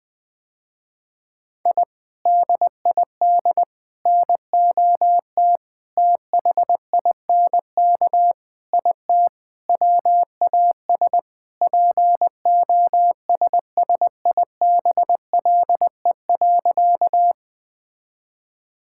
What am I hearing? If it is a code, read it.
Morse code, heard as a keyed tone, about 20 wpm, 700 Hz. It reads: I DID NOT THINK IT WAS POSSIBLE.
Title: Sentences with 4-Word Phrases - Spoken Beginning - 20wpm